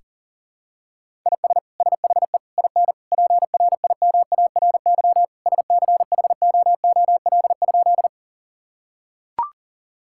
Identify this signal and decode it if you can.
Morse code, heard as a keyed tone, about 40 wpm, 700 Hz. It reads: IS SHE IN PRIMARY SCHOOL?
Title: Sentences from Top 1000 Words - Review Part 5 - 40wpm